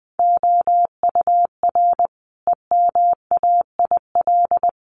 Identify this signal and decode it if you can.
Morse code, heard as a keyed tone, about 20 wpm, 700 Hz. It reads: OUR EMAIL
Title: Sets of 5 Words Encore2 20wpm